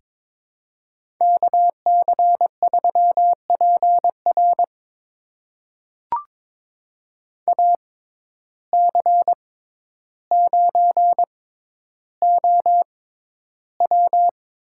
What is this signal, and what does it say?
Morse code, heard as a keyed tone, about 22 wpm, spaced as for 11 wpm, 700 Hz. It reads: KC3PR AC9OW
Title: USA Call Signs - 5 Characters - 6x Character Spacing - 22wpm